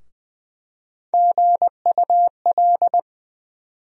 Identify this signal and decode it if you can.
Morse code, heard as a keyed tone, about 20 wpm, 700 Hz. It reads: GUL
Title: USA Call Sign Postfixes - 3 Letters 20wpm